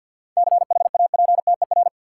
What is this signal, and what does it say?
Morse code, heard as a keyed tone, about 50 wpm, 700 Hz. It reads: CHAPTER
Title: Seven Letter Words - 50wpm